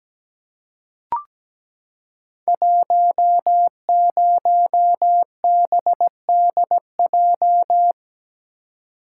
Morse code, heard as a keyed tone, about 17 wpm, 700 Hz. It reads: 10BDJ
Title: Code Groups - 5 Letter:Numbers - 17wpm